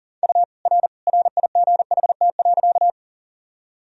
Morse code, heard as a keyed tone, about 40 wpm, 700 Hz. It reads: U R RIGHT.
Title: Sentences from Top 200 Words and Abbreviations 40wpm